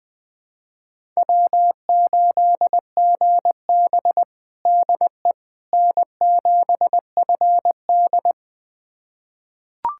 Morse code, heard as a keyed tone, about 20 wpm, 700 Hz. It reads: W8GB DE N7FD
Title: USA Callsign DE Callsign - 4 Characters - 20wpm